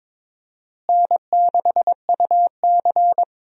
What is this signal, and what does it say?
Morse code, heard as a keyed tone, about 22 wpm, 700 Hz. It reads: N6UC
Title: USA Call Signs - 4 Characters - Repeated 3x - 22wpm